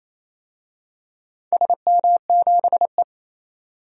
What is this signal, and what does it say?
Morse code, heard as a keyed tone, about 27 wpm, 700 Hz. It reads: SM7E
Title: DXCC-100 Call Signs - 4 Characters - 28wpm